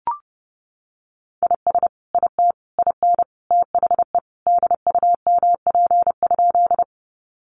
Morse code, heard as a keyed tone, about 30 wpm, 700 Hz. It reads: IS IT IN THE DUMP?
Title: Sentences from Top 1300 Words 30wpm